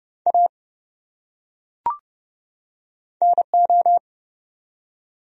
Morse code, heard as a keyed tone, about 30 wpm, 700 Hz. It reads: A NO
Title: Top 100 Words 30wpm